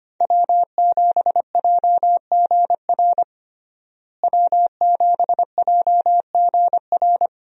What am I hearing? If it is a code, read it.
Morse code, heard as a keyed tone, about 25 wpm, 700 Hz. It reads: W7JGR W7JGR
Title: USA Call Signs - 5 Characters - Repeated 3x - 25wpm